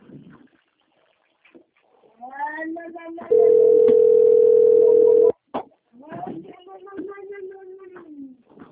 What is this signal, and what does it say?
Ringback tone (North America), 440 + 480 Hz, with speech.
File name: out- - - - - .22286